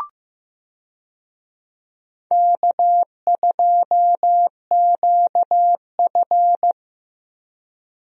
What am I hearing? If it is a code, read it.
Morse code, heard as a keyed tone, about 15 wpm, 700 Hz. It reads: K2QF
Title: USA Call Signs - 4 Characters - Repeated 2x - 15wpm